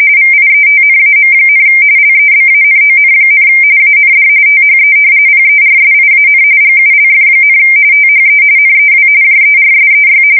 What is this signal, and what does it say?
RTTY, about 45.45 baud. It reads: VIETNAM, LEBANON, AND THE PERSIAN GULF, THE NEW JERSEY
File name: fsk170 / battleshipnj